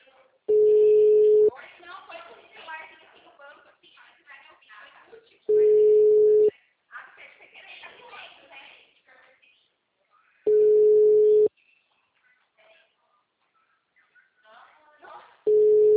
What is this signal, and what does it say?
Ringback tone (Europe, 425 Hz), with speech.